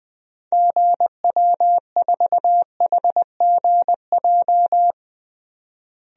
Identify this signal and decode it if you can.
Morse code, heard as a keyed tone, about 20 wpm, 700 Hz. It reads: GW4HGJ
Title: DXCC-100 Call Signs - 6 Characters - Repeated 2x - 20wpm